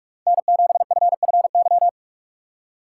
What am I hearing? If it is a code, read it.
Morse code, heard as a keyed tone, about 46 wpm, 700 Hz. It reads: N7FFY
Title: USA Call Signs - 5 Characters - 3x Character Spacing - 45wpm